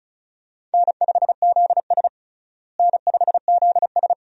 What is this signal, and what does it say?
Morse code, heard as a keyed tone, about 36 wpm, 700 Hz. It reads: N5ZS N5ZS
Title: USA Call Signs - 4 Characters - Repeated 2x - 35wpm